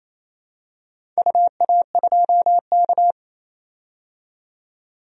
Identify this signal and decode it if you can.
Morse code, heard as a keyed tone, about 28 wpm, 700 Hz. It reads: UA2K
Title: DXCC-100 Call Signs - 4 Characters - 28wpm